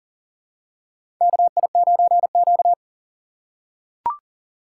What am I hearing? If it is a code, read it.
Morse code, heard as a keyed tone, about 37 wpm, 700 Hz. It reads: KI9Q
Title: USA Call Signs - 4 Characters - 2x Character Spacing - 40wpm